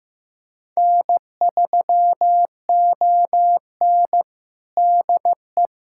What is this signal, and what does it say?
Morse code, heard as a keyed tone, about 15 wpm, 700 Hz. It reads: N3ON DE
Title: USA Callsign DE Callsign - 4 Characters - 15wpm